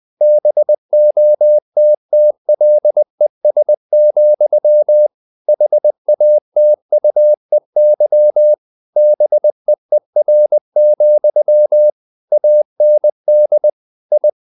Morse code, heard as a keyed tone, about 20 wpm, 600 Hz. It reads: BOTTLES, HATUEY BEER, AND I
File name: the old man and the sea / the old man and the sea part 1 20wpm